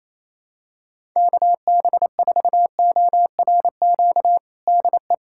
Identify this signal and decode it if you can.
Morse code, heard as a keyed tone, about 28 wpm, 700 Hz. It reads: KB4ORQ DE
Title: USA Callsign DE Callsign - 6 Characters - 28wpm